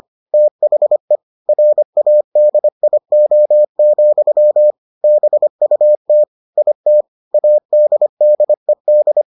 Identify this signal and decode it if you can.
Morse code, heard as a keyed tone, about 25 wpm, 600 Hz. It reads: THE RADIO, BUT IT ADDED